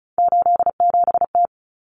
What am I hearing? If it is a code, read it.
Morse code, heard as a keyed tone, about 34 wpm, 700 Hz. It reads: 87T
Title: Common Cut Numbers - 3-Digit 35wpm